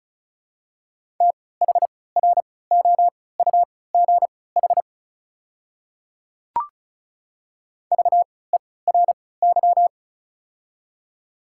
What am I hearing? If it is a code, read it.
Morse code, heard as a keyed tone, about 36 wpm, spaced as for 20 wpm, 700 Hz. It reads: THROUGH VERY
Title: 3x Character Spacing - Top 200 Words - 35wpm